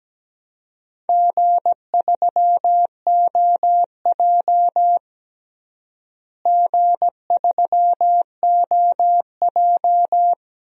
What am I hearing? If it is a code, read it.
Morse code, heard as a keyed tone, about 17 wpm, 700 Hz. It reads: G3OJ G3OJ
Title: DXCC-100 Call Signs - 4 Characters - Repeated 2x - 17wpm